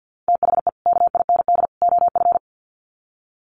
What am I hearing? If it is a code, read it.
Morse code, heard as a keyed tone, about 48 wpm, 700 Hz. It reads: THE KIND OF
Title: Sets of 3 Words 50wpm